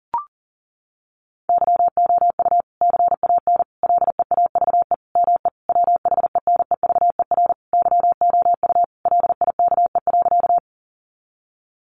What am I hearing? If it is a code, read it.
Morse code, heard as a keyed tone, about 37 wpm, 700 Hz. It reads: YOU CAN LEAVE ME WHENEVER YOU LIKE.
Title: Sentences from Top 1200 Words 40wpm